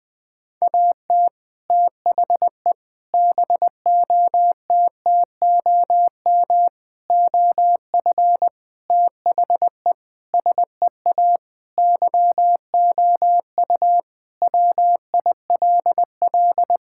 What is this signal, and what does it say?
Morse code, heard as a keyed tone, about 20 wpm, 700 Hz. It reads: AT THE BOTTOM OF THE SEA YOU WILL
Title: Sentences with 3-Word Phrases - Spoken Beginning - 20wpm